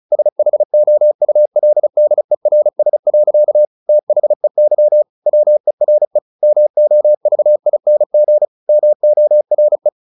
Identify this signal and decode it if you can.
Morse code, heard as a keyed tone, about 35 wpm, 600 Hz. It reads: SHOULDERS. THEY WERE MOVING MORE
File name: the old man and the sea / the old man and the sea 1 35wpm